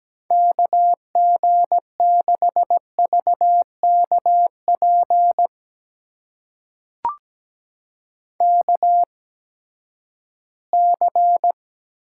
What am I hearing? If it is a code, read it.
Morse code, heard as a keyed tone, about 17 wpm, spaced as for 12 wpm, 700 Hz. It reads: KG6VKP KC
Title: USA Call Signs - 6 Characters - 8x Character Spacing - 17wpm